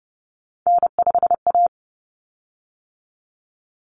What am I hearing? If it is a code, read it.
Morse code, heard as a keyed tone, about 30 wpm, 700 Hz. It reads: N5A
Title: Common Cut Numbers - 3-Digit 30wpm